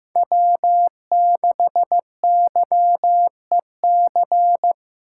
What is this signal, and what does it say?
Morse code, heard as a keyed tone, about 18 wpm, 700 Hz. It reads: W6YEC
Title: Speed Racing - USA 1x3 Call Signs 15wpm